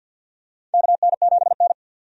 Morse code, heard as a keyed tone, about 48 wpm, 700 Hz. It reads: KN7N
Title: USA Call Signs - 4 Characters - 50wpm